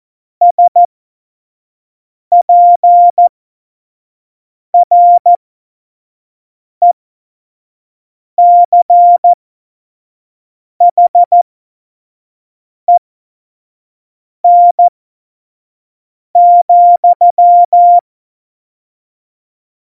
Morse code, heard as a keyed tone, about 14 wpm, spaced as for 5 wpm, 700 Hz. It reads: SPRECHEN,
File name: Tempo 25 F70